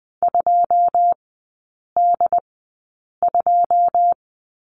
Morse code, heard as a keyed tone, about 20 wpm, spaced as for 8 wpm, 700 Hz. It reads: 2D2
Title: 081 Lesson 14e - Number 2 - Three Characters 20wpm